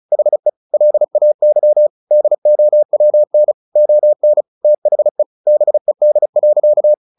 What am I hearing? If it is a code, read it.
Morse code, heard as a keyed tone, about 34 wpm, 600 Hz. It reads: HE LAY DOWN ON THE BED.
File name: the old man and the sea 2 35wpm